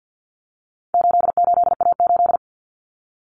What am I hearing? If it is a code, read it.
Morse code, heard as a keyed tone, about 50 wpm, 700 Hz. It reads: 88A8
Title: Common Cut Numbers - 4-Digit 50wpm